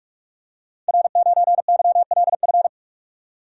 Morse code, heard as a keyed tone, about 44 wpm, 700 Hz. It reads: R9YLF
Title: Code Groups - 5 Letter:Numbers - 45wpm